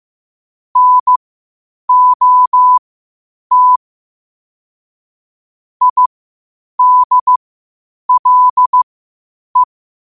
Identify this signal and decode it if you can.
Morse code, heard as a keyed tone, about 15 wpm, spaced as for 8 wpm, 1000 Hz. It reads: NOT IDLE
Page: qotd-at-8-wpm